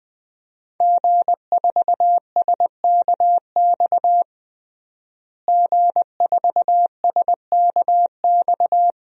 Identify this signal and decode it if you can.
Morse code, heard as a keyed tone, about 20 wpm, 700 Hz. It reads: G4SKX G4SKX
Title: DXCC-100 Call Signs - 5 Characters - Repeated 2x - 20wpm